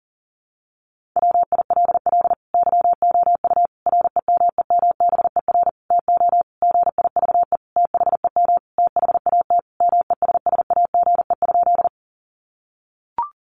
Morse code, heard as a keyed tone, about 40 wpm, 700 Hz. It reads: WILL YOU REMEMBER TO GIVE THEM THAT MESSAGE?
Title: Sentences from Top 1100 Words 40wpm